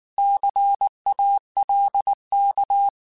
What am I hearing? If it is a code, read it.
Morse code, heard as a keyed tone, about 19 wpm, 800 Hz. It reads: CALK